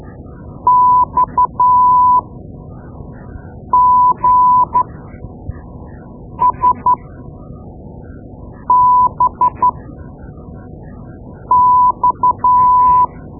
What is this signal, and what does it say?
Morse code, heard as a keyed tone, about 12 wpm, spaced as for 4 wpm, 1000 Hz. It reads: XGSBX
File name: Morse Lessons / Morse Lessons / Tape-2B